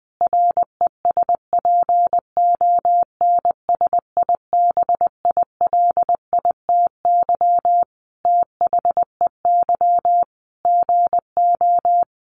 Morse code, heard as a keyed tone, about 20 wpm, 700 Hz. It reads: RESPONSIBILITY THEY GO
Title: Sentences from Top 1200 Words 20wpm